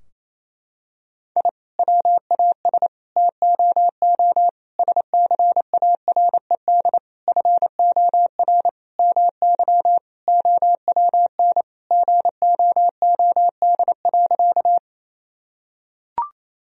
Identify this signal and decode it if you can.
Morse code, heard as a keyed tone, about 28 wpm, 700 Hz. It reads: I WAS TOO SCARED FOR MY OWN GOOD.
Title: Sentences from Top 1000 Words - Review Part 4 - 28wpm